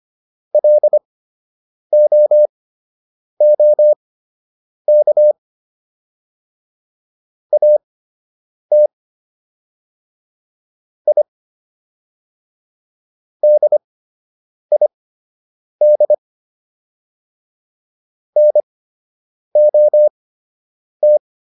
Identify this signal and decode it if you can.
Morse code, heard as a keyed tone, about 25 wpm, spaced as for 8 wpm, 600 Hz. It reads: LOOK AT I DID NOT